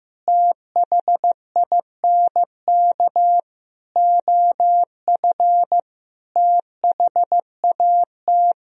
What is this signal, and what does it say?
Morse code, heard as a keyed tone, about 15 wpm, 700 Hz. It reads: THINK OF THAT
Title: Sentences from Top 1000 Words - Review Part 2 - 15wpm